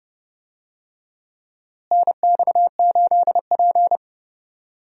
Morse code, heard as a keyed tone, about 30 wpm, 700 Hz. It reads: NX8P
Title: USA Call Signs - 4 Characters - Repeated 2x - 30wpm